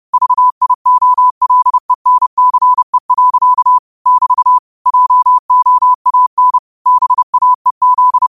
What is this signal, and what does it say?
Morse code, heard as a keyed tone, about 30 wpm, 1000 Hz. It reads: VIOLENCE. <BT> JOAN BAEZ
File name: qotdfiles / 30wpm